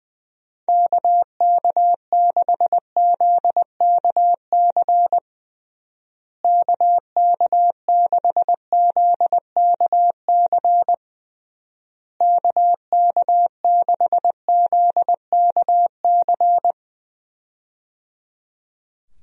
Morse code, heard as a keyed tone, about 19 wpm, 700 Hz. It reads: KK6ZKC KK6ZKC KK6ZKC
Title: USA Call Signs - 6 Characters - Repeated 3x - 20wpm